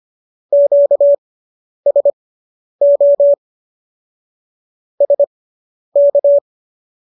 Morse code, heard as a keyed tone, about 25 wpm, spaced as for 10 wpm, 570 Hz. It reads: QSO SK